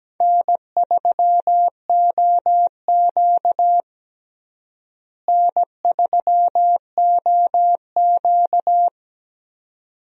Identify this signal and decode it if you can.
Morse code, heard as a keyed tone, about 17 wpm, 700 Hz. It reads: N3OQ N3OQ
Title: USA Call Signs - 4 Characters - Repeated 2x - 17wpm